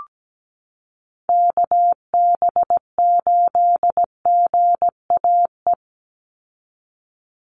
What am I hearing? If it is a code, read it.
Morse code, heard as a keyed tone, about 17 wpm, 700 Hz. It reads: KB8GAE
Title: Call Signs Repeated 3x - 17wpm